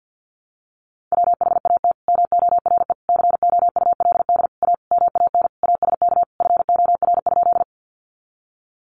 Morse code, heard as a keyed tone, about 50 wpm, 700 Hz. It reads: WHAT MORE COULD A MAN ASK FOR?
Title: Sentences from Top 200 Words 50wpm